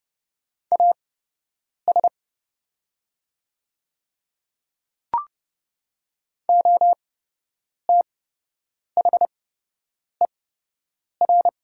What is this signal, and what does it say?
Morse code, heard as a keyed tone, about 29 wpm, spaced as for 8 wpm, 700 Hz. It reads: AS OTHER
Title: 8x Character Spacing - Top 100 Words - 30wpm